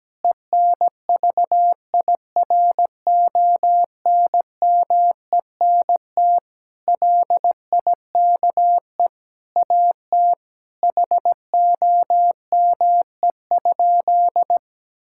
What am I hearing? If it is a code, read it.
Morse code, heard as a keyed tone, about 17 wpm, 700 Hz. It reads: ENVIRONMENT LIKE AT HOME?
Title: Sentences from Top 1000 Words - Review Part 5 - 17wpm